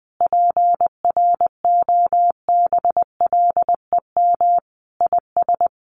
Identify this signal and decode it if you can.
Morse code, heard as a keyed tone, about 20 wpm, 700 Hz. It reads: PROBLEM IS
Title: Sets of 5 Words Encore2 20wpm